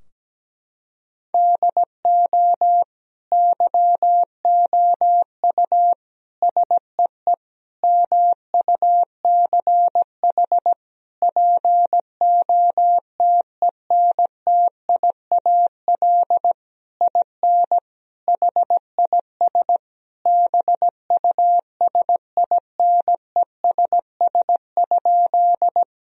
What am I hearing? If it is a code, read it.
Morse code, heard as a keyed tone, about 17 wpm, 700 Hz. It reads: DO YOU SEE MUCH POTENTIAL IN HIS BUSINESS?
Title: Sentences from Top 1000 Words - Review Part 4 - 17wpm